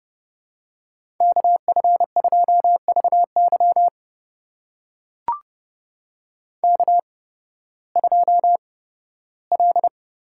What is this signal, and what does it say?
Morse code, heard as a keyed tone, about 29 wpm, spaced as for 16 wpm, 700 Hz. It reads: KF2VY K2L
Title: USA Call Signs - 8x Character Spacing - 30wpm